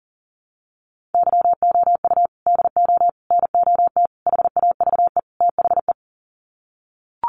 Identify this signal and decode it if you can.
Morse code, heard as a keyed tone, about 40 wpm, 700 Hz. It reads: YOU DO NOT HAVE THE
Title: Sets of 5 Words 40wpm